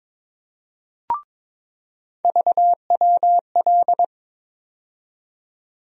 Morse code, heard as a keyed tone, about 22 wpm, 700 Hz. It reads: VWL
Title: USA Call Sign Postfixes - 3 Letters 22wpm